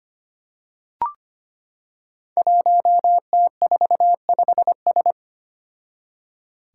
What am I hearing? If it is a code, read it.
Morse code, heard as a keyed tone, about 25 wpm, 700 Hz. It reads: 1T45S
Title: Five Challenging Letter-Number Combinations 25wpm